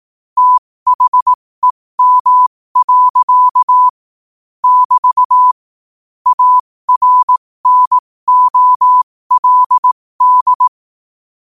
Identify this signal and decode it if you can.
Morse code, heard as a keyed tone, about 18 wpm, spaced as for 15 wpm, 1000 Hz. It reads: THEM. <BT> ARNOLD
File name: qotdfiles / 15wpm